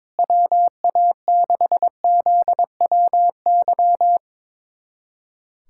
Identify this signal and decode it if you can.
Morse code, heard as a keyed tone, about 22 wpm, 700 Hz. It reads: WA6ZWY
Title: USA Callsign DE Callsign - 6 Characters - 22wpm